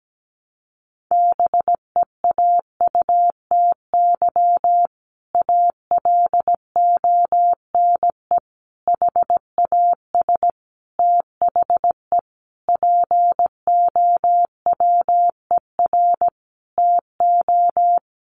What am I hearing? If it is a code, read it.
Morse code, heard as a keyed tone, about 17 wpm, 700 Hz. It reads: BEAUTY ALONE HAS THE POWER TO
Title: Sentences from Top 1200 Words 17wpm